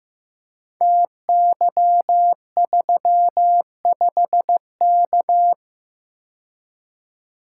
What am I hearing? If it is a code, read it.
Morse code, heard as a keyed tone, about 15 wpm, 700 Hz. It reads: TY35K
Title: Code Groups - 5 Letter:Numbers - 15wpm